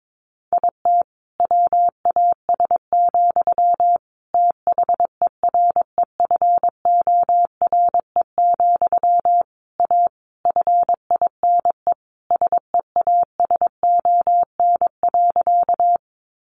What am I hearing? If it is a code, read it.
Morse code, heard as a keyed tone, about 22 wpm, 700 Hz. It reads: IT WAS, THEREFORE, A FINE SEASON.
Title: Sentences from Top 1400 Words 22wpm